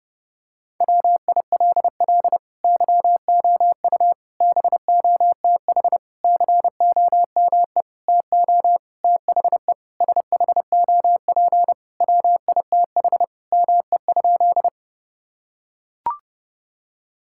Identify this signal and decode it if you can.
Morse code, heard as a keyed tone, about 32 wpm, 700 Hz. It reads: WILL YOU BOTH COME TO THE SHOP WITH ME?
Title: Sentences from Top 1600 Words 30wpm